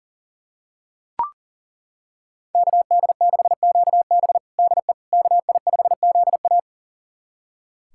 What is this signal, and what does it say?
Morse code, heard as a keyed tone, about 40 wpm, 700 Hz. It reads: KD6QB DE KI5ZA
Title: USA Callsign DE Callsign - 5 Characters - 40wpm